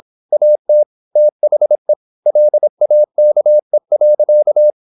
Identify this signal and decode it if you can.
Morse code, heard as a keyed tone, about 26 wpm, 600 Hz. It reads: AT THE LAKE.
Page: Never 26wpm | CW med Gnister